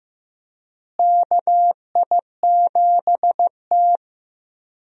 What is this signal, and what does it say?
Morse code, heard as a keyed tone, about 15 wpm, 700 Hz. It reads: KI7T
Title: USA Call Signs - 4 Characters - Repeated 3x - 15wpm